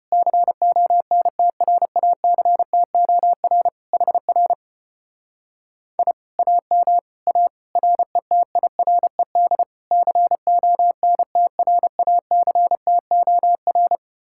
Morse code, heard as a keyed tone, about 33 wpm, 700 Hz. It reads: CONTRACTOR HR I AM A RETIRED CONTRACTOR